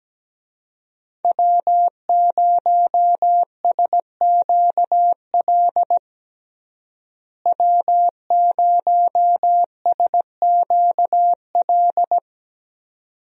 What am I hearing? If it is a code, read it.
Morse code, heard as a keyed tone, about 17 wpm, 700 Hz. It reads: W0SQL W0SQL
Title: USA Call Signs - 5 Characters - Repeated 2x - 17wpm